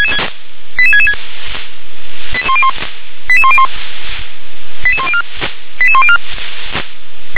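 Five-tone selective call (ZVEI), heard as quick runs of five tones, tones of about 70 ms. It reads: 88666 88101 88104 88104
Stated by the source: ZVEI five-tone selective call received by SDR.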